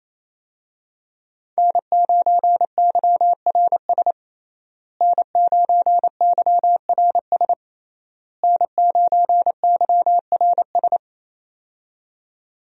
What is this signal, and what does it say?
Morse code, heard as a keyed tone, about 27 wpm, 700 Hz. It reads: N9YRS N9YRS N9YRS
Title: USA Call Signs - 5 Characters - Repeated 3x - 28wpm